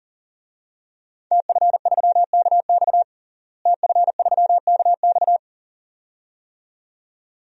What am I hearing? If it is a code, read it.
Morse code, heard as a keyed tone, about 37 wpm, 700 Hz. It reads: TF3KX TF3KX
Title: DXCC-100 Call Signs - 5 Characters - Repeated 2x - 40wpm